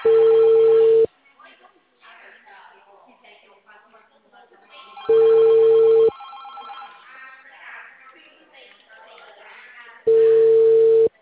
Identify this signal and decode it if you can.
Ringback tone (China, 450 Hz), with speech.
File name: out- -2044- - - .24483